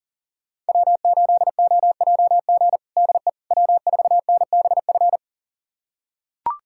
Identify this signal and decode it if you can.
Morse code, heard as a keyed tone, about 40 wpm, 700 Hz. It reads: W8OJG DE W4NBF
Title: USA Callsign DE Callsign - 5 Characters - 40wpm